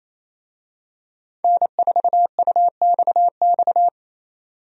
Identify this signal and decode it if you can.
Morse code, heard as a keyed tone, about 28 wpm, 700 Hz. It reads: N4UXX
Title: USA Call Signs - 5 Characters - 28wpm